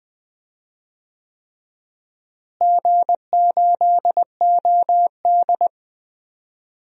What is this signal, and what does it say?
Morse code, heard as a keyed tone, about 20 wpm, 700 Hz. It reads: G8OD
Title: DXCC-100 Call Signs - 4 Characters - Repeated 3x - 20wpm